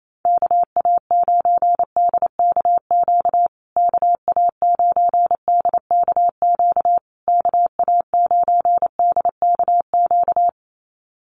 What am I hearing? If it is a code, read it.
Morse code, heard as a keyed tone, about 28 wpm, 700 Hz. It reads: KA9DKQ KA9DKQ KA9DKQ
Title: USA Call Signs Repeated 3x 28wpm